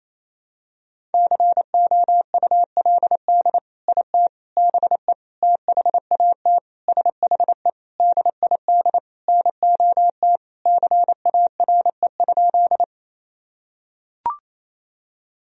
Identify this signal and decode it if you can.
Morse code, heard as a keyed tone, about 28 wpm, 700 Hz. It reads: COULD IT BE THAT SHE DID NOT CARE?
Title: Sentences from Top 1000 Words - Review Part 1 - 28wpm